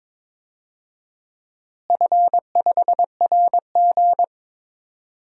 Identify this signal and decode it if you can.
Morse code, heard as a keyed tone, about 22 wpm, 700 Hz. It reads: F5RG
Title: DXCC-100 Call Signs - 4 Characters - Repeated 3x - 22wpm